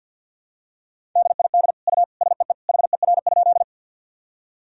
Morse code, heard as a keyed tone, about 52 wpm, 700 Hz. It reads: DID U SEE HER?
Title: Sentences from Top 100 Words and Abbreviations 50wpm